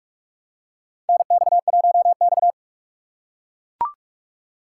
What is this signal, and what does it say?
Morse code, heard as a keyed tone, about 44 wpm, 700 Hz. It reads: NX1X
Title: USA Call Signs - 6x Character Spacing - 45wpm